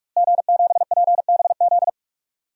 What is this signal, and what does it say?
Morse code, heard as a keyed tone, about 43 wpm, 700 Hz. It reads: G7PBZ
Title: DXCC-100 Call Signs - 5 Characters - Repeated 3x - 45wpm